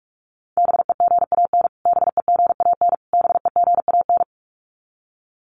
Morse code, heard as a keyed tone, about 44 wpm, 700 Hz. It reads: BEGAN BEGAN BEGAN
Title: Top 500 Words Repeated 45wpm